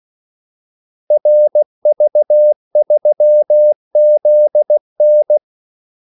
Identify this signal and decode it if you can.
Morse code, heard as a keyed tone, about 16 wpm, 600 Hz. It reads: RV3ZN
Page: Kaldesignaler 16 wpm